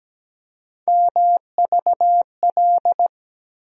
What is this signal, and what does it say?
Morse code, heard as a keyed tone, about 17 wpm, 700 Hz. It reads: MVL
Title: USA Call Sign Postfixes - 3 Letters 17wpm